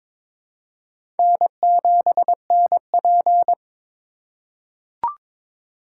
Morse code, heard as a keyed tone, about 22 wpm, 700 Hz. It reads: N7NP
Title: USA Call Signs - 4 Characters - 3x Character Spacing - 22wpm